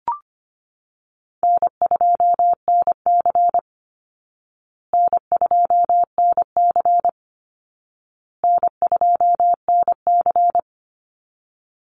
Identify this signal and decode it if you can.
Morse code, heard as a keyed tone, about 25 wpm, 700 Hz. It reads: N2NC N2NC N2NC
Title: Easy Call Signs Repeated 3x - 25wpm